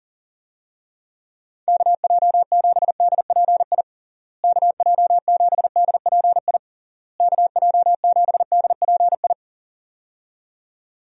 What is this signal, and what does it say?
Morse code, heard as a keyed tone, about 40 wpm, 700 Hz. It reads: KJ7DPI KJ7DPI KJ7DPI
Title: USA Call Signs - 6 Characters - Repeated 3x - 40wpm